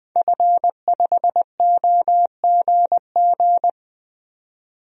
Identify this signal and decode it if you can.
Morse code, heard as a keyed tone, about 21 wpm, 700 Hz. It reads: F5OGG
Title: DXCC-100 Call Signs - 5 Characters - 20wpm